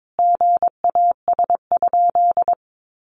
Morse code, heard as a keyed tone, about 22 wpm, 700 Hz. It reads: GAS?
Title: Sentences from Top 1100 Words 22wpm